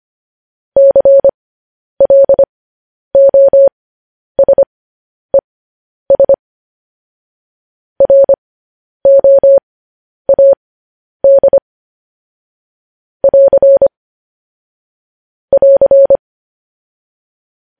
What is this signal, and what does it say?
Morse code, heard as a keyed tone, about 25 wpm, spaced as for 10 wpm, 550 Hz. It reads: CLOSES ROAD <AR> <AR>
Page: news headlines in morse code at 10 wpm